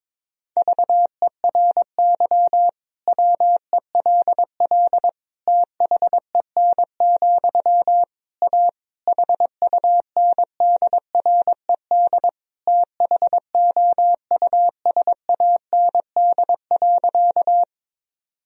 Morse code, heard as a keyed tone, about 21 wpm, 700 Hz. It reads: VERY WELL THEN, A HUNDRED THOUSAND.
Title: Sentences from Top 1000 Words - Review Part 3 - 22wpm